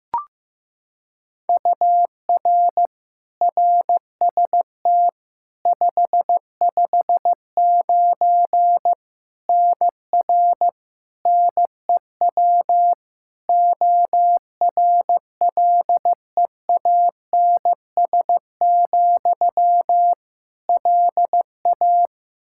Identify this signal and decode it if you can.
Morse code, heard as a keyed tone, about 15 wpm, 700 Hz. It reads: UR RST 559 NR NEW ORLEANS, LA
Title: K1USN SST QSO Practice 15wpm